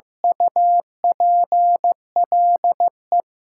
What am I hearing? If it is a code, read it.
Morse code, heard as a keyed tone, about 16 wpm, 700 Hz. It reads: UPLE
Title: Sets of 2 Words - New Word Drill - 15wpm